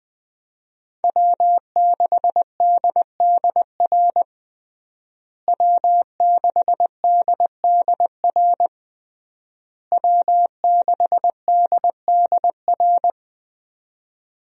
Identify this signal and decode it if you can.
Morse code, heard as a keyed tone, about 20 wpm, 700 Hz. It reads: W6DDR W6DDR W6DDR
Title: USA Call Signs - 5 Characters - Repeated 3x - 20wpm